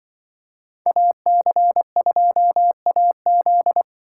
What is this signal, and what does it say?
Morse code, heard as a keyed tone, about 24 wpm, 700 Hz. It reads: AC2AZ
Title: Speed Racing - USA 2x2 Call Signs 20wpm